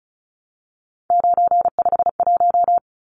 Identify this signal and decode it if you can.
Morse code, heard as a keyed tone, about 34 wpm, 700 Hz. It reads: 951
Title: Numbers - 3-Digit 35wpm